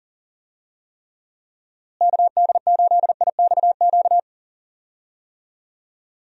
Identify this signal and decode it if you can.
Morse code, heard as a keyed tone, about 39 wpm, 700 Hz. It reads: KD8IXQ
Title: USA Call Signs - 6 Characters - Repeated 2x - 40wpm